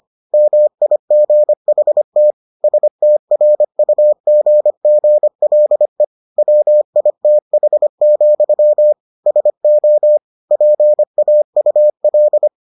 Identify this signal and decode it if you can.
Morse code, heard as a keyed tone, about 25 wpm, 600 Hz. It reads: MIGHT STRUGGLE WITH, SO PAUL